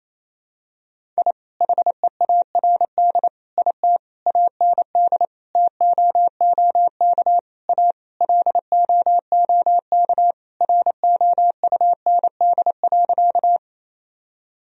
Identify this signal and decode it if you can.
Morse code, heard as a keyed tone, about 28 wpm, 700 Hz. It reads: I HEARD IT AND TOOK A LOOK ROUND.
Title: Sentences from Top 1000 Words - Review Part 1 - 28wpm